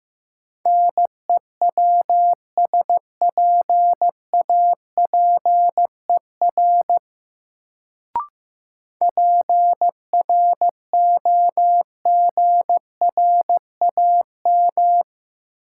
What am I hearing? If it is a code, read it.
Morse code, heard as a keyed tone, about 15 wpm, 700 Hz. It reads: NEWSPAPER PROGRAM
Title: Sentences from Top 1000 Words - Review Part 4 - New Word Drill - 15wpm